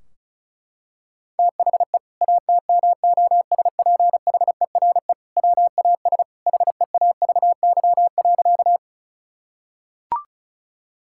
Morse code, heard as a keyed tone, about 36 wpm, 700 Hz. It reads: THE ATMOSPHERE WAS HEAVY.
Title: Sentences from Top 1600 Words 35wpm